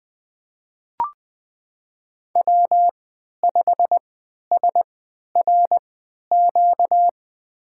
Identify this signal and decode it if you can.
Morse code, heard as a keyed tone, about 20 wpm, spaced as for 11 wpm, 700 Hz. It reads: W5SRQ
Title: USA Call Signs - 5 Characters - 3x Character Spacing - 20wpm